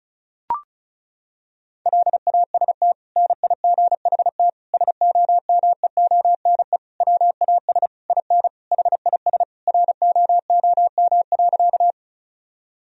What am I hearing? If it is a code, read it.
Morse code, heard as a keyed tone, about 36 wpm, 700 Hz. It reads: LAST NIGHT SOMEONE WAS IN HIS ROOM.
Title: Sentences from Top 1000 Words - Review Part 4 - 35wpm